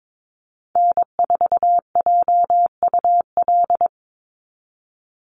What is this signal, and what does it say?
Morse code, heard as a keyed tone, about 22 wpm, 700 Hz. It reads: N4JUL
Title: USA Call Signs 22wpm